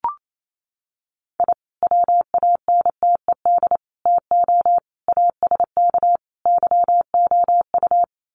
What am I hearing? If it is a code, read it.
Morse code, heard as a keyed tone, about 28 wpm, 700 Hz. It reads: I WANTED TO ASK YOU
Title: Sets of 5 Words Encore2 28wpm